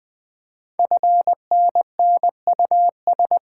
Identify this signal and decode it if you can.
Morse code, heard as a keyed tone, about 20 wpm, 700 Hz. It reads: FNNUS
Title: Code Groups - 5 Letters - 20wpm